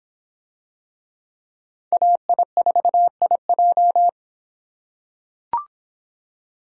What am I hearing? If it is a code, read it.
Morse code, heard as a keyed tone, about 25 wpm, 700 Hz. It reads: AI4IJ